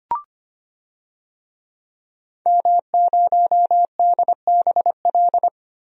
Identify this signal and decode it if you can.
Morse code, heard as a keyed tone, about 25 wpm, 700 Hz. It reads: M0DBL
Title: DXCC-100 Call Signs - 5 Characters - 25wpm